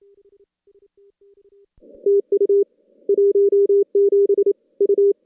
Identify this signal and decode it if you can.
Morse code, heard as a keyed tone, about 27 wpm, 400 Hz. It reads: TU 17 U
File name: contests / CW-SS-k5tr- -2100